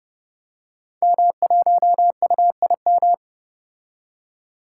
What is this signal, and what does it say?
Morse code, heard as a keyed tone, about 28 wpm, 700 Hz. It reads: M1UIM
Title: Code Groups - 5 Letter:Numbers - 30wpm